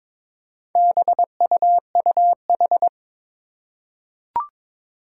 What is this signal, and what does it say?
Morse code, heard as a keyed tone, about 22 wpm, 700 Hz. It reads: BUUH
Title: Four Challenging Letter-Number Combinations 22wpm